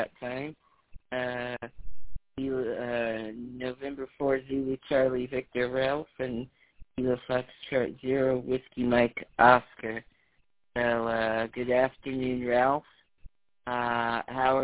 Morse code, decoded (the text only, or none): none